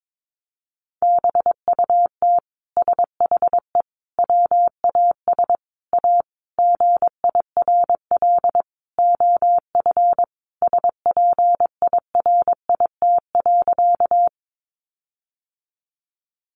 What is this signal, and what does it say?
Morse code, heard as a keyed tone, about 22 wpm, 700 Hz. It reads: BUT SHE WAS A GIRL OF SPIRIT.
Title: Sentences from Top 1400 Words 22wpm